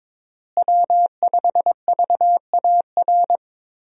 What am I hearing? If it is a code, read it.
Morse code, heard as a keyed tone, about 23 wpm, 700 Hz. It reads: W5VAR
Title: USA Call Signs - 5 Characters - Repeated 3x - 22wpm